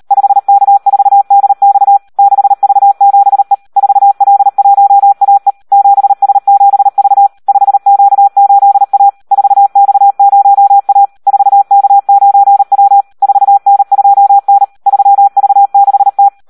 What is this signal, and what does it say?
Morse code, heard as a keyed tone, about 38 wpm, 800 Hz. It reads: 5K4DX 6V7E 4L1AE 7S7V 5Q8A 4X0A 4K9W 4N2N 3V6T
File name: Audio practica CW / SD 1 38